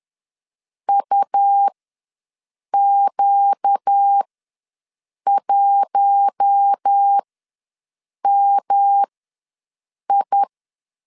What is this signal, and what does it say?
Morse code, heard as a keyed tone, about 10 wpm, spaced as for 6 wpm, 800 Hz. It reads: UQ1MI
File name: LEZIONI / 5 LBVC / 5 L ws 10wpm 3636